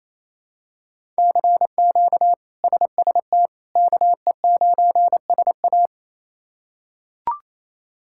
Morse code, heard as a keyed tone, about 28 wpm, 700 Hz. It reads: CQ SST KE9SA
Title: K1USN SST QSO Practice - 28wpm